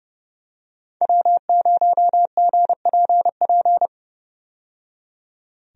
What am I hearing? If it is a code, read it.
Morse code, heard as a keyed tone, about 30 wpm, 700 Hz. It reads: W0GPP
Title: USA Call Signs - 5 Characters - 30wpm